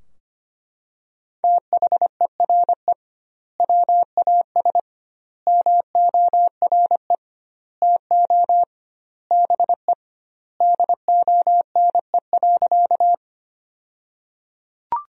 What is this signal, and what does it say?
Morse code, heard as a keyed tone, about 25 wpm, 700 Hz. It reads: THERE WAS MORE TO BE DONE.
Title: 2x Word-Spacing - Sentences from Top 500 Words - 25wpm